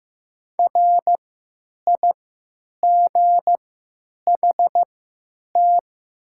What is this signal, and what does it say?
Morse code, heard as a keyed tone, about 15 wpm, spaced as for 9 wpm, 700 Hz. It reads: RIGHT
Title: 3x Character Spacing - Top 200 Words - 15wpm